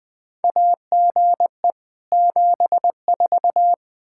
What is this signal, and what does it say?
Morse code, heard as a keyed tone, about 20 wpm, 700 Hz. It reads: AGE 74
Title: Practice QSO - Exchange 3 - More Info - 20wpm